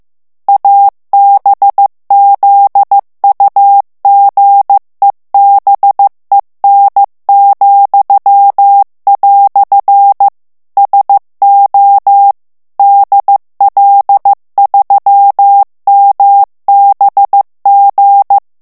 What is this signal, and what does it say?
Morse code, heard as a keyed tone, about 15 wpm, 800 Hz. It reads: ABZUGEBEN," SO DL3MBG
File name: morse mp3 / ALLEMAND texte clair 1 vitesse 14 mots minute